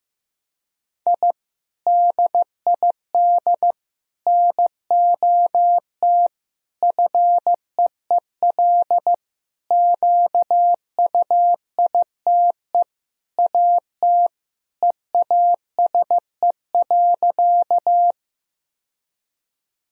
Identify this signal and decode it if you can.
Morse code, heard as a keyed tone, about 15 wpm, 700 Hz. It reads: I DID NOT FEEL QUITE AT EASE.
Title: Sentences from Top 1000 Words - Review Part 1 - 15wpm